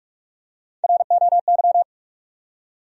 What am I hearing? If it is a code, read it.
Morse code, heard as a keyed tone, about 46 wpm, 700 Hz. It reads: ROY
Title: USA Common Names - Max 4 Letters - 45wpm